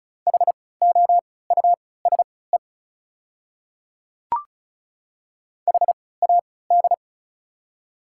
Morse code, heard as a keyed tone, about 34 wpm, spaced as for 20 wpm, 690 Hz. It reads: HOUSE HAD
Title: 3x Character Spacing - Top 300 Words - 35wpm